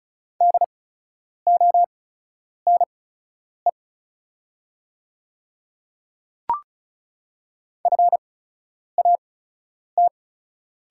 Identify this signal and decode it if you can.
Morse code, heard as a keyed tone, about 36 wpm, spaced as for 10 wpm, 700 Hz. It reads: DONE FAT
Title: 8x Character Spacing - Top 500 Words - 35wpm